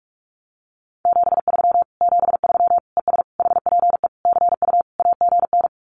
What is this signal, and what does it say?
Morse code, heard as a keyed tone, about 44 wpm, 700 Hz. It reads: 73 73 ES HPE CU AGN
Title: Common QSO Phrases Encore 45wpm